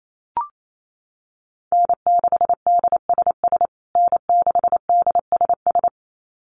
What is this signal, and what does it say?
Morse code, heard as a keyed tone, about 28 wpm, 700 Hz. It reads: N6DSS N6DSS
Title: USA Call Signs Repeated 2x 28wpm